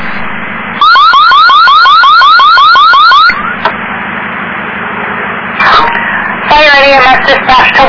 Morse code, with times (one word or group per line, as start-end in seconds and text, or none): none